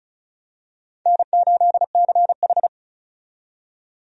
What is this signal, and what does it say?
Morse code, heard as a keyed tone, about 34 wpm, 700 Hz. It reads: N8CH
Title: USA Call Signs - 4 Characters - 35wpm